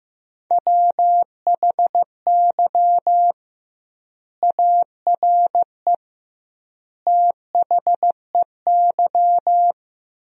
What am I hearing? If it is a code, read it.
Morse code, heard as a keyed tone, about 15 wpm, 700 Hz. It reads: WHY ARE THEY
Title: 2x Word-Spacing - Sentences from Top 200 Words - 15wpm